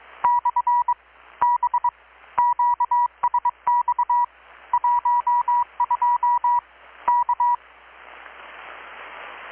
Morse code, heard as a keyed tone, about 22 wpm, 970 Hz. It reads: / B QSX 12 K
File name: blog / swl / 6-abril-2019-12695-USB-KPS